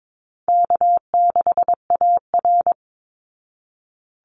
Morse code, heard as a keyed tone, about 22 wpm, 700 Hz. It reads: K6AR
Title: Call Signs Repeated 2x - 22wpm